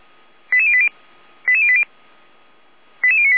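Five-tone selective call (ZVEI), heard as quick runs of five tones, tones of about 70 ms. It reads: 88080 88080 88098
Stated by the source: ZVEI five-tone selective call received on an SDR.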